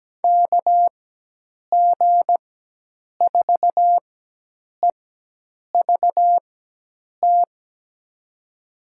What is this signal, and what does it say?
Morse code, heard as a keyed tone, about 17 wpm, spaced as for 8 wpm, 700 Hz. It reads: KG4EVT
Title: USA Call Signs - 4x Character Spacing - 17wpm